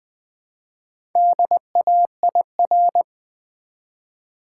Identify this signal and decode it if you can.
Morse code, heard as a keyed tone, about 20 wpm, 700 Hz. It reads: DAIR
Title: Build A Word - 1 Letter At A Time - 5-Letter Words - 20wpm